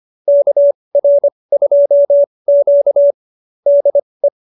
Morse code, heard as a keyed tone, about 25 wpm, spaced as for 20 wpm, 570 Hz. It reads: KR2Q DE
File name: qso220b 20